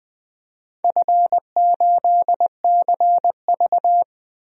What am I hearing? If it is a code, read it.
Morse code, heard as a keyed tone, about 21 wpm, 700 Hz. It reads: F8CV
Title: DXCC-100 Call Signs - 4 Characters - 20wpm